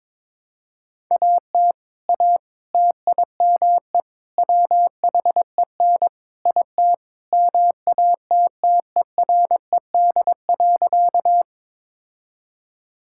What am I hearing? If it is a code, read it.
Morse code, heard as a keyed tone, about 22 wpm, 700 Hz. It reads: AT A TIME WHEN IT MATTERED.
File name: practice / 22wpm / Sentences with 4-Word Phrases - 22wpm